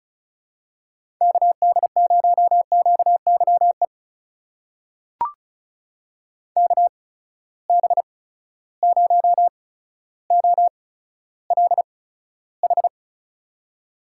Morse code, heard as a keyed tone, about 36 wpm, spaced as for 15 wpm, 700 Hz. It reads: KD0QYE KB0OLH
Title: USA Call Signs - 6 Characters - 8x Character Spacing - 35wpm